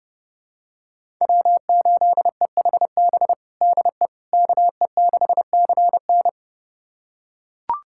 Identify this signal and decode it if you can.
Morse code, heard as a keyed tone, about 31 wpm, 700 Hz. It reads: W8EHB DE KE6CN
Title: USA Callsign DE Callsign - 5 Characters - 30wpm